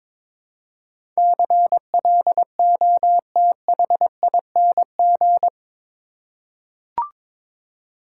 Morse code, heard as a keyed tone, about 21 wpm, 700 Hz. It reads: CLOTHING
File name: practice / 22wpm / Eight Letter Words - 22wpm